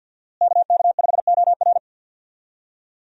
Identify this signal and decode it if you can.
Morse code, heard as a keyed tone, about 53 wpm, 700 Hz. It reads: KK5OR
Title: USA Call Signs - 5 Characters - Repeated 3x - 50wpm